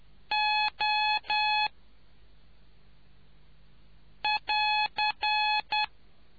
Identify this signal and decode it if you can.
Morse code, heard as a keyed tone, about 10 wpm, 850 Hz. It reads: O <AR>